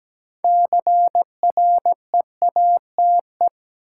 Morse code, heard as a keyed tone, about 17 wpm, 700 Hz. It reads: CREATE
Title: Sentences from Top 1000 Words - Review Part 3 - New Word Drill - 17wpm